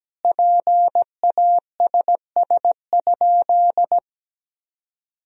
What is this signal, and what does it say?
Morse code, heard as a keyed tone, about 17 wpm, 700 Hz. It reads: PASS?
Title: Sentences from Top 400 Words and Abbreviations 17wpm